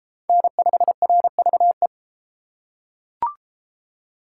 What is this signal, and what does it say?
Morse code, heard as a keyed tone, about 33 wpm, 700 Hz. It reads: N5RVE